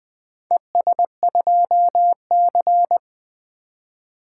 Morse code, heard as a keyed tone, about 25 wpm, 700 Hz. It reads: ES2C
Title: Speed-Racing - Four Letter-Number Combinations - 20wpm